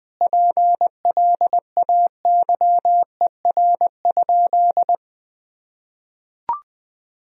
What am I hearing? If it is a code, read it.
Morse code, heard as a keyed tone, about 20 wpm, 700 Hz. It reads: PLAYER?
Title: Sentences from Top 1000 Words - Review Part 4 - 20wpm